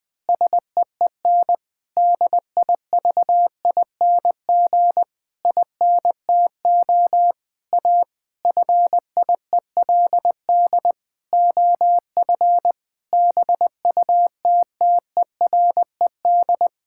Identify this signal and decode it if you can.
Morse code, heard as a keyed tone, about 20 wpm, 700 Hz. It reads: SEEN DIVING INTO A FIELD OF BUTTERED
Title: Pangram Challenge v2 - 20wpm